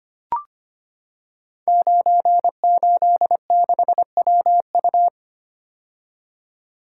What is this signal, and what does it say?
Morse code, heard as a keyed tone, about 25 wpm, 700 Hz. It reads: 986WU
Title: Code Groups - 5 Letter:Numbers - 25wpm